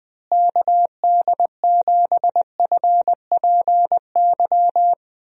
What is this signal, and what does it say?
Morse code, heard as a keyed tone, about 20 wpm, 700 Hz. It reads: KD7FPY
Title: USA Call Signs - 6 Characters - Repeated 3x - 20wpm